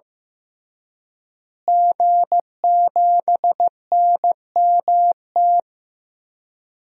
Morse code, heard as a keyed tone, about 15 wpm, 700 Hz. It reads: G7NMT
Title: DXCC-100 Call Signs - 5 Characters - Repeated 3x - 15wpm